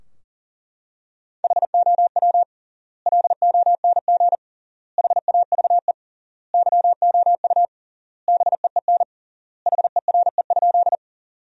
Morse code, heard as a keyed tone, about 38 wpm, 700 Hz. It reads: HOW LONG HAVE YOU BEEN HERE?
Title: 3x Word-Spacing - Sentences from Top 200 Words - 40wpm